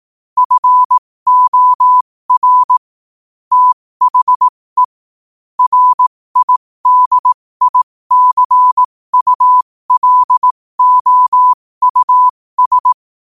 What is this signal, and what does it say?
Morse code, heard as a keyed tone, about 18 wpm, spaced as for 15 wpm, 1000 Hz. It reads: FOR THE RIDICULOUS
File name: 15wpm